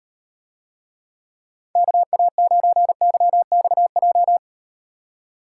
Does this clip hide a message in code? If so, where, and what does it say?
Yes: Morse code from 1.8 to 4.4 s, heard as a keyed tone, about 38 wpm, 700 Hz. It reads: KA9YXJ